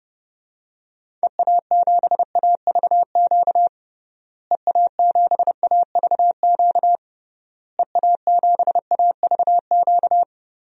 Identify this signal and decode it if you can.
Morse code, heard as a keyed tone, about 30 wpm, 700 Hz. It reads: EA7AVQ EA7AVQ EA7AVQ
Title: DXCC-100 Call Signs - 6 Characters - Repeated 3x - 30wpm